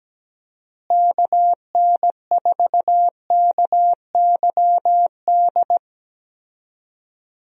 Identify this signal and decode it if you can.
Morse code, heard as a keyed tone, about 17 wpm, 700 Hz. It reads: KN4KYD
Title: USA Call Signs - 6 Characters - Repeated 2x - 17wpm